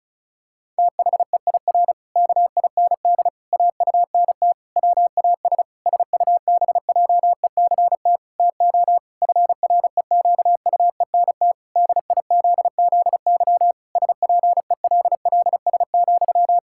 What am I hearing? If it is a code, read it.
Morse code, heard as a keyed tone, about 36 wpm, 700 Hz. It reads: THEIR KIND AUNT WAS SUBJECT TO FREQUENT DIZZY SPELLS,
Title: Pangram Challenge v2 - 35wpm